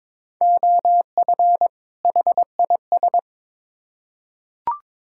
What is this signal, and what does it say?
Morse code, heard as a keyed tone, about 22 wpm, 700 Hz. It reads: OF HIS
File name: Build A Phrase - 2-Word Phrases - 22wpm